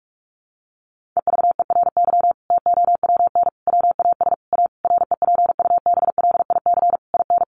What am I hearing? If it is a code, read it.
Morse code, heard as a keyed tone, about 45 wpm, 700 Hz. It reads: EVERY TOWN WAS A REPUBLIC IN
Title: Sentences from Top 1500 Words 45wpm